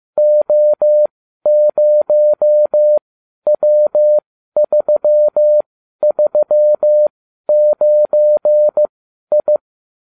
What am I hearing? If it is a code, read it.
Morse code, heard as a keyed tone, about 15 wpm, spaced as for 12 wpm, 600 Hz. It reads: O0W339I